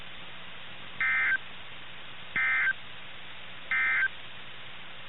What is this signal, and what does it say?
Emergency Alert System (SAME) tones, heard as harsh buzzing bursts. It is heard from 0.9 to 4.2 s.
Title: NOAA Weather Radio - KIH61